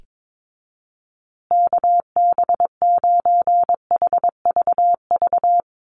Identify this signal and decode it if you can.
Morse code, heard as a keyed tone, about 22 wpm, 700 Hz. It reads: KB9HVV
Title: USA Call Signs Repeated 3x 22wpm